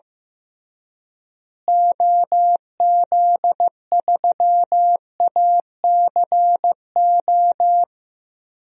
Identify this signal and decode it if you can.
Morse code, heard as a keyed tone, about 15 wpm, 700 Hz. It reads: OZ3ACO
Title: DXCC-100 Call Signs - 6 Characters - Repeated 3x - 15wpm